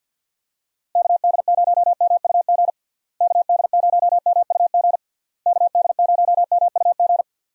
Morse code, heard as a keyed tone, about 51 wpm, 700 Hz. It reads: KD0MUG KD0MUG KD0MUG
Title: USA Call Signs - 6 Characters - Repeated 3x - 50wpm